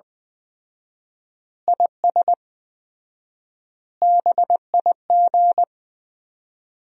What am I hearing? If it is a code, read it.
Morse code, heard as a keyed tone, about 19 wpm, 700 Hz. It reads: IS BIG
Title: 4x Word-Spacing - Sentences from Top 400 Words - 20wpm